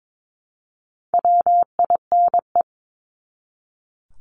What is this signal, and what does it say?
Morse code, heard as a keyed tone, about 22 wpm, 700 Hz. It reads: WINE
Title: Top 1501-1750 Words 22wpm